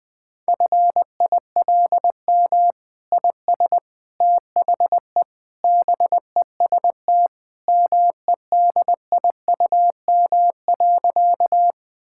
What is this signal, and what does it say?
Morse code, heard as a keyed tone, about 20 wpm, 700 Hz. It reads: FILM IS THE BEST MEDIUM.
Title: Sentences from Top 1000 Words - Review Part 6 - 20wpm